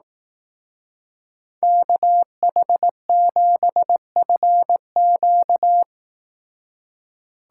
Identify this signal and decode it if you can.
Morse code, heard as a keyed tone, about 18 wpm, 700 Hz. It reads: KH7FQ
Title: Speed Racing - USA 2x2 Call Signs 15wpm